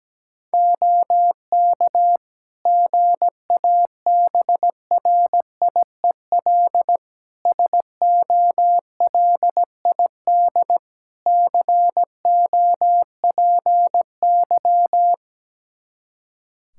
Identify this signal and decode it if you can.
Morse code, heard as a keyed tone, about 17 wpm, 700 Hz. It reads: OK GABRIEL SOLID COPY
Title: Practice QSO - Exchange 3 - More Info - 17wpm